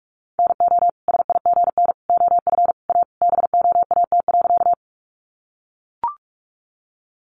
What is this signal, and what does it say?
Morse code, heard as a keyed tone, about 44 wpm, 700 Hz. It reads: NO SIGN OF A BOAT.
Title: Sentences from Top 1000 Words 45wpm